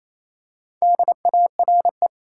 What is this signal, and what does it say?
Morse code, heard as a keyed tone, about 28 wpm, 700 Hz. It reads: DARE
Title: Four Letter Words - 28wpm